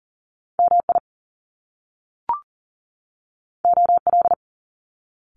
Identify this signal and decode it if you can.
Morse code, heard as a keyed tone, about 39 wpm, 700 Hz. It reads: MI OL
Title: Top Two Letter Combinations in Words 40wpm